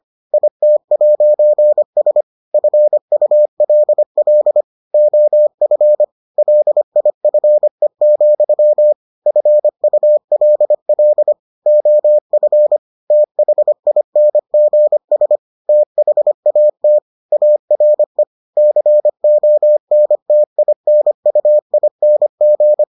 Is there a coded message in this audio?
Yes: Morse code from 0.3 to 22.8 s, heard as a keyed tone, about 25 wpm, 600 Hz. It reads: IT'S FULL OF LIFE, FULL OF THINGS THAT ARE CONTINUING